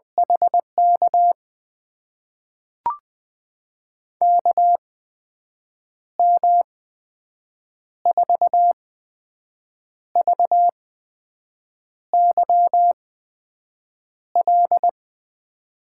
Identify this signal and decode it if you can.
Morse code, heard as a keyed tone, about 20 wpm, spaced as for 6 wpm, 700 Hz. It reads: HK KM4VYL
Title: USA Call Signs - 6 Characters - 8x Character Spacing - 20wpm